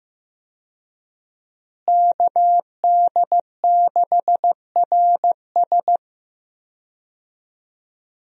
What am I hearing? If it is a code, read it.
Morse code, heard as a keyed tone, about 15 wpm, 700 Hz. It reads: KD6RS
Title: USA Call Signs - 5 Characters - Repeated 3x - 15wpm